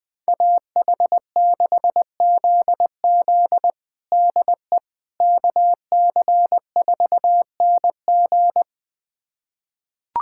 Morse code, heard as a keyed tone, about 20 wpm, 700 Hz. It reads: AH6ZZ DE KC4NG
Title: USA Callsign DE Callsign - 5 Characters - 20wpm